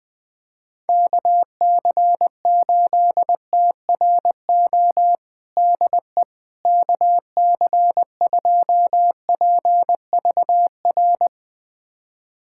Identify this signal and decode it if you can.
Morse code, heard as a keyed tone, about 20 wpm, 700 Hz. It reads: KC8TRO DE KC2PVR
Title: USA Callsign DE Callsign - 6 Characters - 20wpm